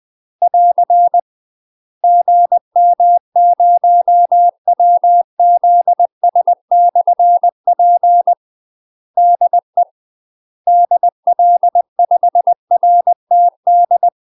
Morse code, heard as a keyed tone, about 20 wpm, 700 Hz. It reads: <AR> GM0WZS/P DE DL5RTD
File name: UK MASTER QSOs TRAINING / 20WPM / UK MASTER QSOs 0068